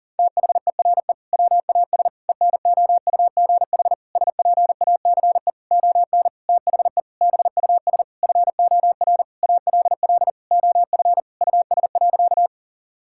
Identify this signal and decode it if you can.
Morse code, heard as a keyed tone, about 37 wpm, 700 Hz. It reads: THERE WAS ENOUGH SPACE ON THE BUS FOR ALL OF US.
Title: Sentences from Top 1000 Words - Review Part 6 - 40wpm